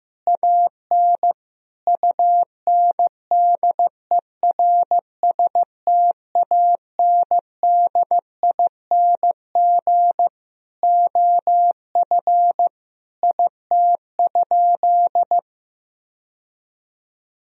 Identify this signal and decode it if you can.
Morse code, heard as a keyed tone, about 15 wpm, 700 Hz. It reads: AN UNDERSTANDING OF IT?
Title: Sentences from Top 1000 Words - Review Part 3 - 15wpm